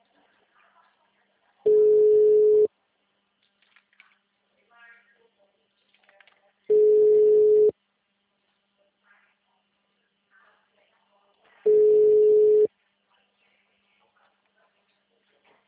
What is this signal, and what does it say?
Ringback tone (Europe, 425 Hz).